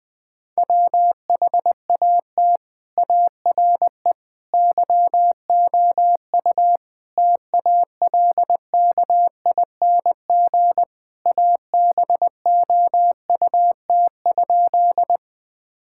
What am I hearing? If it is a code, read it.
Morse code, heard as a keyed tone, about 21 wpm, 700 Hz. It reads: WHAT ARE YOU TALKING ABOUT?
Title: Sentences with 5-Word Phrases - 20wpm